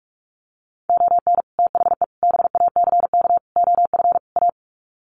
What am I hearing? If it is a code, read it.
Morse code, heard as a keyed tone, about 44 wpm, 700 Hz. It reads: ON THE BACK OF A
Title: Sets of 5 Words Encore2 45wpm